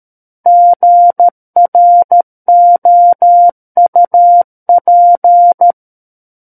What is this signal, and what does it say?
Morse code, heard as a keyed tone, about 13 wpm, 700 Hz. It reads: GROUP
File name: Book One / 04-Chapter 4 - 13 WPM